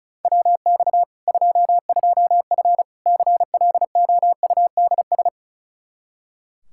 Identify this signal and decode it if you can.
Morse code, heard as a keyed tone, about 36 wpm, 700 Hz. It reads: WX 22F CLOUDS
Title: Practice QSO - Exchange 2 - Info - 35wpm